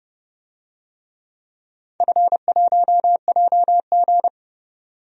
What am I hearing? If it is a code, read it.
Morse code, heard as a keyed tone, about 30 wpm, 700 Hz. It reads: F1JG
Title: DXCC-100 Call Signs - 4 Characters - 30wpm